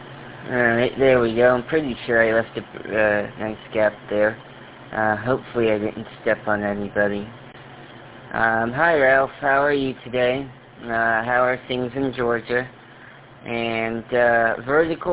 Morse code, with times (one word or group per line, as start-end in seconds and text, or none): none